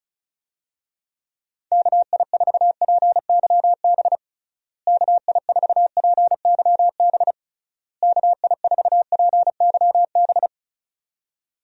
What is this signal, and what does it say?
Morse code, heard as a keyed tone, about 34 wpm, 700 Hz. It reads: KI4PYB KI4PYB KI4PYB
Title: USA Call Signs - 6 Characters - Repeated 3x - 35wpm